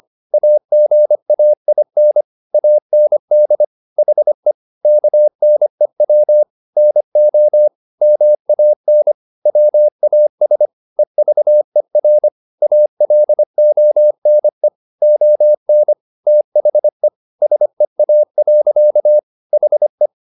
Morse code, heard as a keyed tone, about 25 wpm, 600 Hz. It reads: AGAIN AND HE KNEW NO MAN WAS EVER ALONE ON THE SEA. HE
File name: the old man and the sea part 1 25wpm